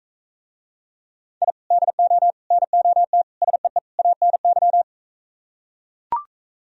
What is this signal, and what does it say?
Morse code, heard as a keyed tone, about 42 wpm, 700 Hz. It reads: I DO NOT SEE ANY